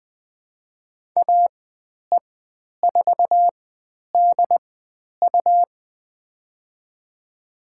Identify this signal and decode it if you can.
Morse code, heard as a keyed tone, about 20 wpm, spaced as for 10 wpm, 700 Hz. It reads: AE4DU
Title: 102 Lesson 17g - Letter u - Call Signs 20wpm